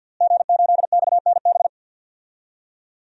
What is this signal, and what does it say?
Morse code, heard as a keyed tone, about 46 wpm, 700 Hz. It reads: G8XNB
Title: DXCC-100 Call Signs - 5 Characters - Repeated 2x - 50wpm